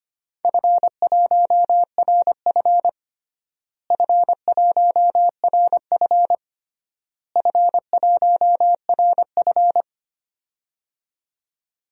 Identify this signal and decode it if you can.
Morse code, heard as a keyed tone, about 25 wpm, 700 Hz. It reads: F1RF F1RF F1RF
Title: DXCC-100 Call Signs - 4 Characters - Repeated 3x - 25wpm